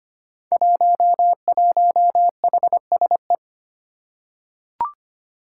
Morse code, heard as a keyed tone, about 25 wpm, 700 Hz. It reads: 11HSE
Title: Five Challenging Letter-Number Combinations 25wpm